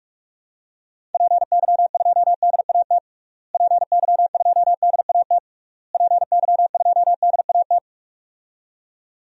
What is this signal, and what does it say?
Morse code, heard as a keyed tone, about 46 wpm, 700 Hz. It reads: PY2DAT PY2DAT PY2DAT
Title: DXCC-100 Call Signs - 6 Characters - Repeated 3x - 45wpm